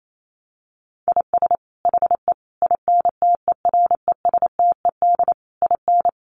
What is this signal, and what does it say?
Morse code, heard as a keyed tone, about 28 wpm, 700 Hz. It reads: IS HE INTERESTED IN
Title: Sentences from Top 900 Words 28wpm